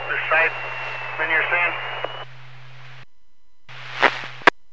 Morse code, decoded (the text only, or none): none